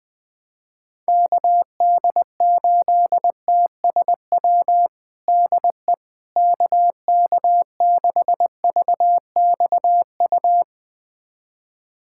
Morse code, heard as a keyed tone, about 20 wpm, 700 Hz. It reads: KD8TSW DE KK6VXU
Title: USA Callsign DE Callsign - 6 Characters - 20wpm